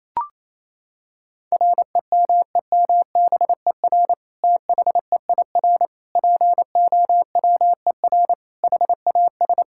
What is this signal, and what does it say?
Morse code, heard as a keyed tone, about 27 wpm, 700 Hz. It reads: REMEMBER THEIR POWER HAS
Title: Sentences from Top 1000 Words - Review Part 5 - 28wpm